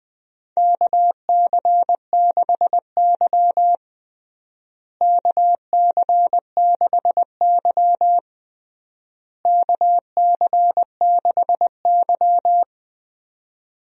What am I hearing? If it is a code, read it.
Morse code, heard as a keyed tone, about 20 wpm, 700 Hz. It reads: KC6Y KC6Y KC6Y
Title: USA Call Signs - 4 Characters - Repeated 3x - 20wpm